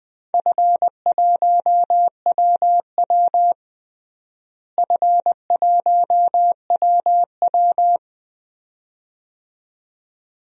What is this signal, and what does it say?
Morse code, heard as a keyed tone, about 20 wpm, 700 Hz. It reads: F1WW F1WW
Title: DXCC-100 Call Signs - 4 Characters - Repeated 2x - 20wpm